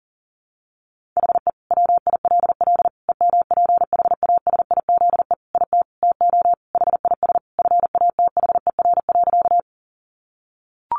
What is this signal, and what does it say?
Morse code, heard as a keyed tone, about 40 wpm, 700 Hz. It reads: HE WILL EMPHASIZE IT TO HIS FATHER.
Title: Sentences from Top 1500 Words 40wpm